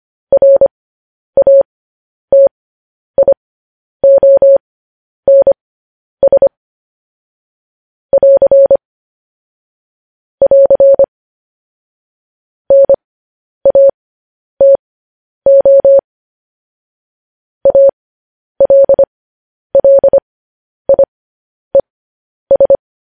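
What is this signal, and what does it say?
Morse code, heard as a keyed tone, about 25 wpm, spaced as for 10 wpm, 550 Hz. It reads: RATIONS <AR> <AR> NATO ALLIES
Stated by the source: Morse code transmission of news headlines.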